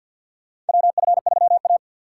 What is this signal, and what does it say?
Morse code, heard as a keyed tone, about 46 wpm, 700 Hz. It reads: WF3A
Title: USA Call Signs - 4 Characters - 2x Character Spacing - 50wpm